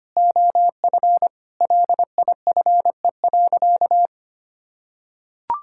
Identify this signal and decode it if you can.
Morse code, heard as a keyed tone, about 25 wpm, 700 Hz. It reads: OF LIFE.
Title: Sentences from Top 1000 Words - Review Part 6 - 25wpm